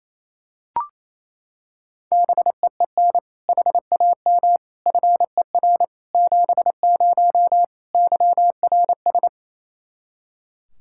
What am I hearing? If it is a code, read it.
Morse code, heard as a keyed tone, about 28 wpm, 700 Hz. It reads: BEEN HAM FER 70 YRS
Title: Practice QSO - Exchange 3 - More Info - 28wpm